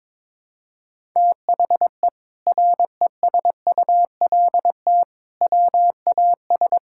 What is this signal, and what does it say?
Morse code, heard as a keyed tone, about 22 wpm, 700 Hz. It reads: THE RESULT WAS
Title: Sentences from Top 1000 Words - Review Part 6 - 22wpm